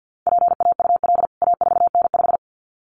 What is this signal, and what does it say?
Morse code, heard as a keyed tone, about 49 wpm, 700 Hz. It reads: PAUL A4N5
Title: CWT Exchange 50wpm